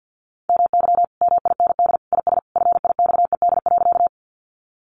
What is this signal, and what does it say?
Morse code, heard as a keyed tone, about 50 wpm, 700 Hz. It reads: MY MIND IS FIXED.
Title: Sentences from Top 1100 Words 50wpm